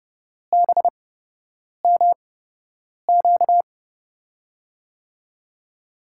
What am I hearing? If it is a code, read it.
Morse code, heard as a keyed tone, about 30 wpm, spaced as for 8 wpm, 700 Hz. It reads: BMQ
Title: USA Call Signs - 8x Character Spacing - 30wpm